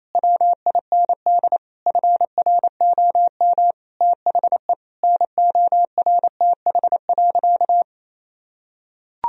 Morse code, heard as a keyed tone, about 28 wpm, 700 Hz. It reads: WIND FROM THE NORTH.
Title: Sentences from Top 1000 Words - Review Part 2 - 28wpm